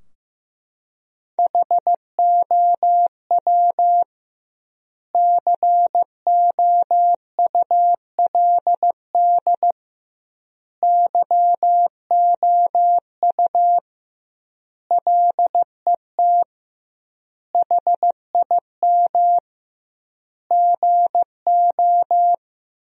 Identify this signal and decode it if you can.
Morse code, heard as a keyed tone, about 15 wpm, 700 Hz. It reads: HOW COULD YOU LET HIM GO
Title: 2x Word-Spacing - Sentences from Top 300 Words - 15wpm